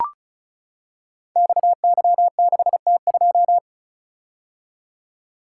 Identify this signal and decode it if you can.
Morse code, heard as a keyed tone, about 36 wpm, 700 Hz. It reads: XY6T2
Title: Code Groups - 5 Letter:Numbers - 35wpm